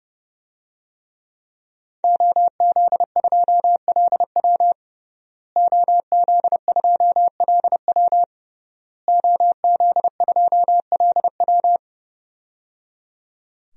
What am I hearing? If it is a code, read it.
Morse code, heard as a keyed tone, about 30 wpm, 700 Hz. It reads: OZ2LW OZ2LW OZ2LW
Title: DXCC-100 Call Signs - 5 Characters - Repeated 3x - 30wpm